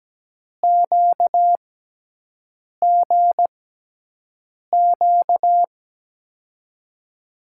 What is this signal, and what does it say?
Morse code, heard as a keyed tone, about 17 wpm, spaced as for 6 wpm, 700 Hz. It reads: QGQ
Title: USA Call Signs - 6 Characters - 6x Character Spacing - 17wpm